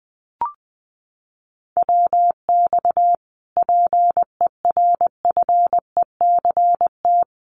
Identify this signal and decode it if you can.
Morse code, heard as a keyed tone, about 20 wpm, 700 Hz. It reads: WX PERFECT
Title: Common QSO Phrases Encore 20wpm